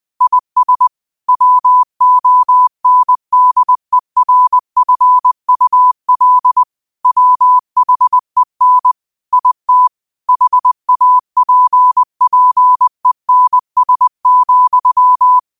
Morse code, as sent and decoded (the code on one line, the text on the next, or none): .. ... / .-- --- -. -.. .-. ..-. ..- .-.. / .-- .... -. / .. - / .... .- .--. .--. -. ... --..--
IS WONDERFUL WHEN IT HAPPENS,